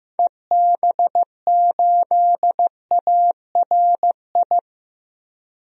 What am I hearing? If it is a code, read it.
Morse code, heard as a keyed tone, about 15 wpm, 700 Hz. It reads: EB8ARI
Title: DXCC-100 Call Signs - 6 Characters - Repeated 3x - 15wpm